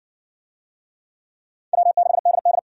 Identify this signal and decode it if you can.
Morse code, heard as a keyed tone, about 57 wpm, 700 Hz. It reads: W6DD